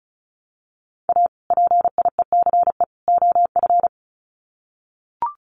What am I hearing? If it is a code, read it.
Morse code, heard as a keyed tone, about 34 wpm, 700 Hz. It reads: A PIECE OF
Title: Sets of 3 Words 35wpm